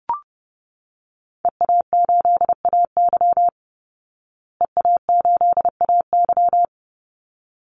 Morse code, heard as a keyed tone, about 30 wpm, 700 Hz. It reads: EA8AY EA8AY
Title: Call Signs Repeated 3x - 30wpm